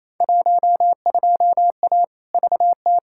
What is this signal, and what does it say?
Morse code, heard as a keyed tone, about 28 wpm, 700 Hz. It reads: 12A VT
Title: Field Day Exchanges Encore - 28wpm